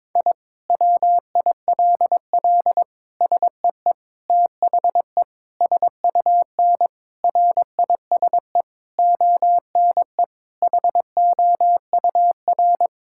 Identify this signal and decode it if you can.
Morse code, heard as a keyed tone, about 22 wpm, 700 Hz. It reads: I WILL SEE THE SUN RISE ONE HOUR
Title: Sentences from Top 1000 Words - Review Part 6 - 22wpm